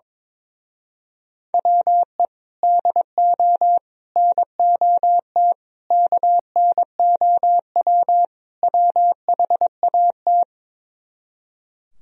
Morse code, heard as a keyed tone, about 23 wpm, 700 Hz. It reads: WE DO NOT KNOW WHAT
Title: Speed-Racing - Sets of 5 Words 22wpm